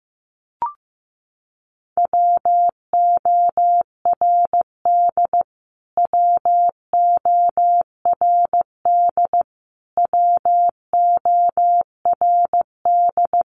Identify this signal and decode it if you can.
Morse code, heard as a keyed tone, about 15 wpm, 700 Hz. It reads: WORD WORD WORD
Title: Top 500 Words Repeated 15wpm